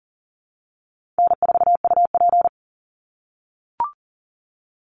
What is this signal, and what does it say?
Morse code, heard as a keyed tone, about 37 wpm, 700 Hz. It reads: N4UP
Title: Easy Call Signs - 40wpm